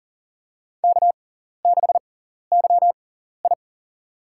Morse code, heard as a keyed tone, about 40 wpm, spaced as for 14 wpm, 700 Hz. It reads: K6YI
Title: USA Call Signs - 4 Characters - 6x Character Spacing - 40wpm